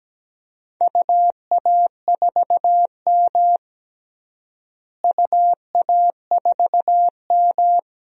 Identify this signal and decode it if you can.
Morse code, heard as a keyed tone, about 17 wpm, 700 Hz. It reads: UA4M UA4M
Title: DXCC-100 Call Signs - 4 Characters - Repeated 2x - 17wpm